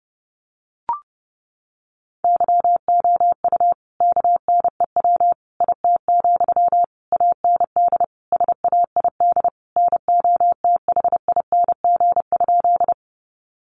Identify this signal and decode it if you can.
Morse code, heard as a keyed tone, about 30 wpm, 700 Hz. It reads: YOU KNEW IT, AND SAID NOTHING?
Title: Sentences from Top 500 Words 30wpm